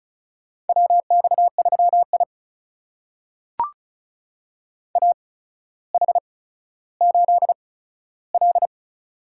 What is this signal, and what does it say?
Morse code, heard as a keyed tone, about 36 wpm, spaced as for 15 wpm, 700 Hz. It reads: WX3I AH8L
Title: USA Call Signs - 8x Character Spacing - 35wpm